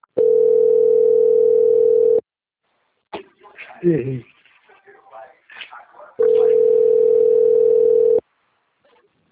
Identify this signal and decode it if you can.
Ringback tone (North America), 440 + 480 Hz, with speech.
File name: out- - - - - .17847